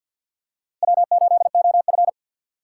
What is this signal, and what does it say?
Morse code, heard as a keyed tone, about 53 wpm, 700 Hz. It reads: W8OF
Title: USA Call Signs - 4 Characters - 6x Character Spacing - 50wpm